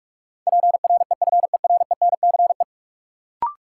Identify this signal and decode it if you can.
Morse code, heard as a keyed tone, about 46 wpm, 700 Hz. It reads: PREFERENCE
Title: Sentences from Top 1500 Words - New Word Drill - 45wpm